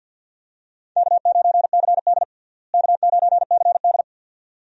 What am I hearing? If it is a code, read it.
Morse code, heard as a keyed tone, about 50 wpm, 700 Hz. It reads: K9CD K9CD
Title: USA Call Signs - 4 Characters - Repeated 2x - 50wpm